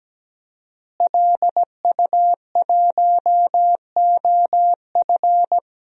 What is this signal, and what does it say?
Morse code, heard as a keyed tone, about 17 wpm, 700 Hz. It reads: LU1OF
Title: DXCC-100 Call Signs - 5 Characters - Repeated 3x - 17wpm